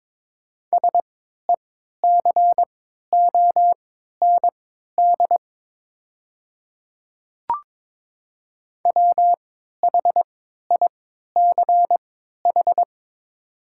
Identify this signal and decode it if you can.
Morse code, heard as a keyed tone, about 22 wpm, spaced as for 12 wpm, 700 Hz. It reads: SECOND WHICH
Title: 3x Character Spacing - Top 300 Words - 22wpm